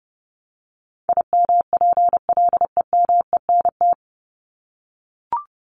Morse code, heard as a keyed tone, about 30 wpm, 700 Hz. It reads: IMPLEMENT
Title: Top 1501-1750 Words 30wpm